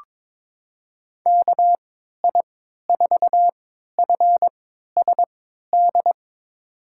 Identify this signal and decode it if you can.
Morse code, heard as a keyed tone, about 22 wpm, spaced as for 12 wpm, 700 Hz. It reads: KI4FSD
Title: USA Call Signs - 6 Characters - 3x Character Spacing - 22wpm